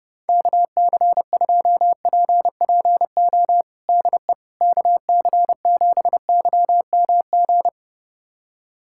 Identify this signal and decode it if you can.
Morse code, heard as a keyed tone, about 32 wpm, 700 Hz. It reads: KC2PPO DE KC7YMG
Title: USA Callsign DE Callsign - 6 Characters - 30wpm